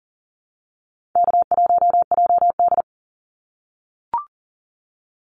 Morse code, heard as a keyed tone, about 40 wpm, 700 Hz. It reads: K1JD
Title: NA SOTA Activator Callsigns 40wpm